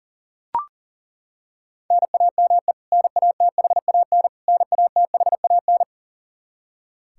Morse code, heard as a keyed tone, about 37 wpm, 700 Hz. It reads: NAME NATHAN NATHAN
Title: Practice QSO - Exchange 1 - Beginning - 40wpm